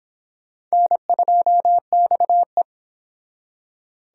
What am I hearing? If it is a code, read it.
Morse code, heard as a keyed tone, about 26 wpm, 700 Hz. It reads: N2XE